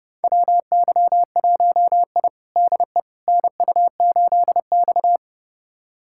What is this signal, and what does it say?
Morse code, heard as a keyed tone, about 31 wpm, 700 Hz. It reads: WY1I DE NU8X
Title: USA Callsign DE Callsign - 4 Characters - 30wpm